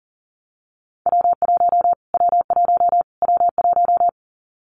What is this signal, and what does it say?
Morse code, heard as a keyed tone, about 40 wpm, 700 Hz. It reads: W1 W1 W1
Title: Callsign Prefixes for Top 25 DXCC - Repeated 3x - 40wpm